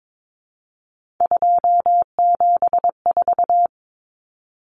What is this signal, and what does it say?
Morse code, heard as a keyed tone, about 22 wpm, 700 Hz. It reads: 274
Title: Numbers - 3-Digit 22wpm